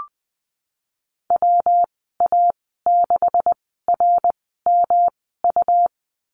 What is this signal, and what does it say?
Morse code, heard as a keyed tone, about 20 wpm, spaced as for 15 wpm, 700 Hz. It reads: WA6RMU
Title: 140 Lesson 23g - Letter f - Call Signs 20wpm